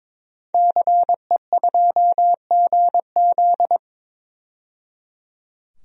Morse code, heard as a keyed tone, about 22 wpm, 700 Hz. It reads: CE2GZ
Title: DXCC-100 Call Signs - 5 Characters - 22wpm